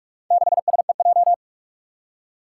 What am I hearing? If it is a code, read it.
Morse code, heard as a keyed tone, about 43 wpm, 700 Hz. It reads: 6SEJ
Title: Four Challenging Letter-Number Combinations 45wpm